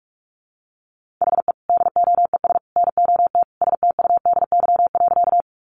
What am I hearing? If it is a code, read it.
Morse code, heard as a keyed tone, about 44 wpm, 700 Hz. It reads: HE DOES NOT STUDY.
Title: Additional Easy Phrases 45wpm